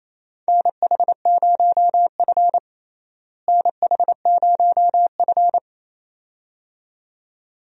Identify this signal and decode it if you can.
Morse code, heard as a keyed tone, about 28 wpm, 700 Hz. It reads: NH0F NH0F
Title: USA Call Signs - 4 Characters - Repeated 2x - 28wpm